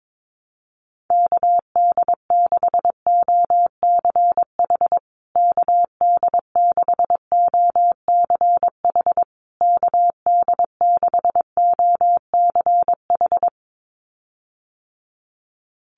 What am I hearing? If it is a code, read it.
Morse code, heard as a keyed tone, about 22 wpm, 700 Hz. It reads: KD6OCH KD6OCH KD6OCH
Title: USA Call Signs Repeated 3x 22wpm